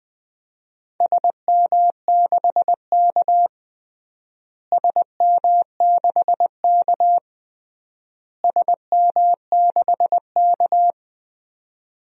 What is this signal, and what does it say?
Morse code, heard as a keyed tone, about 20 wpm, 700 Hz. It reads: SM6K SM6K SM6K
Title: DXCC-100 Call Signs - 4 Characters - Repeated 3x - 20wpm